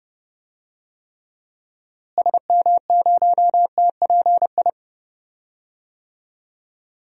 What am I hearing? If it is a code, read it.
Morse code, heard as a keyed tone, about 30 wpm, 700 Hz. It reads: SM0TPI
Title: DXCC-100 Call Signs - 6 Characters - Repeated 2x - 30wpm